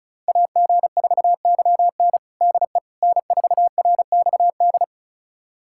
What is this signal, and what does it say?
Morse code, heard as a keyed tone, about 36 wpm, 700 Hz. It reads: AG4YN DE N4RXD
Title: USA Callsign DE Callsign - 5 Characters - 35wpm